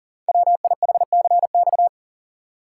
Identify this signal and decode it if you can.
Morse code, heard as a keyed tone, about 41 wpm, 690 Hz. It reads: WIHCX
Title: Code Groups - 5 Letters - 40wpm